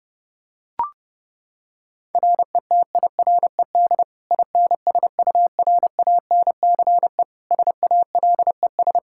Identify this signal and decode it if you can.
Morse code, heard as a keyed tone, about 30 wpm, 700 Hz. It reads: RETIRED INSURANCE SALES
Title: Practice QSO - Exchange 3 - More Info - 30wpm